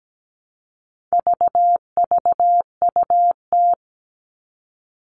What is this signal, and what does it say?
Morse code, heard as a keyed tone, about 17 wpm, 700 Hz. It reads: VVUT
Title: All Cut Numbers - 4-Digit 17wpm